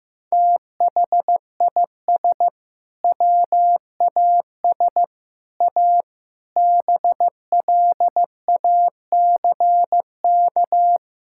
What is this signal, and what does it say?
Morse code, heard as a keyed tone, about 15 wpm, 700 Hz. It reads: THIS WAS A BLACK
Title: Sentences from Top 1600 Words 15wpm